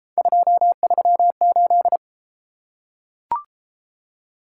Morse code, heard as a keyed tone, about 29 wpm, 700 Hz. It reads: 238